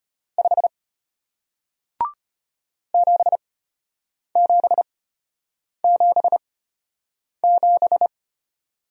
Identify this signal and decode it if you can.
Morse code, heard as a keyed tone, about 34 wpm, spaced as for 8 wpm, 700 Hz. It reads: 5 7777